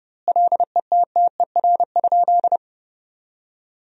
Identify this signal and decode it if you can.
Morse code, heard as a keyed tone, about 30 wpm, 700 Hz. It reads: LETTER?
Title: 2x Word-Spacing - Sentences from Top 300 Words - 30wpm